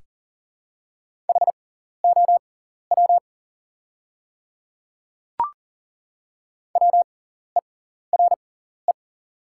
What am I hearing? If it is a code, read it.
Morse code, heard as a keyed tone, about 41 wpm, spaced as for 14 wpm, 690 Hz. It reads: HOW WERE
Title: 6x Character Spacing - Top 300 Words - 40wpm